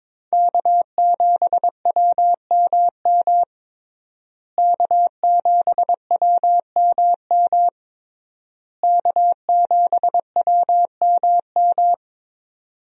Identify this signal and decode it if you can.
Morse code, heard as a keyed tone, about 22 wpm, 700 Hz. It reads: K7WMM K7WMM K7WMM
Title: USA Call Signs - 5 Characters - Repeated 3x - 22wpm